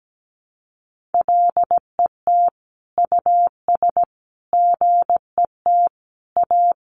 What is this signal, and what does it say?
Morse code, heard as a keyed tone, about 17 wpm, 700 Hz. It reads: LET US GET A
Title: Sentences from Top 1400 Words 17wpm